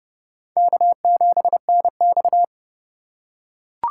Morse code, heard as a keyed tone, about 28 wpm, 700 Hz. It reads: K7NX